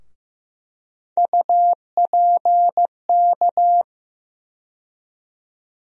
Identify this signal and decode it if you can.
Morse code, heard as a keyed tone, about 15 wpm, 700 Hz. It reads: UPK
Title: USA Call Sign Postfixes - 3 Letters 15wpm